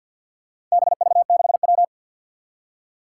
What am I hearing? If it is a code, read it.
Morse code, heard as a keyed tone, about 53 wpm, 700 Hz. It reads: BV6W
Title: Four Challenging Letter-Number Combinations 50wpm